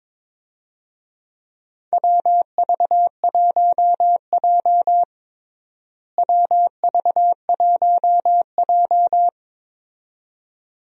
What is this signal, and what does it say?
Morse code, heard as a keyed tone, about 22 wpm, 700 Hz. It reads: WV1J WV1J
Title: USA Call Signs - 4 Characters - Repeated 2x - 22wpm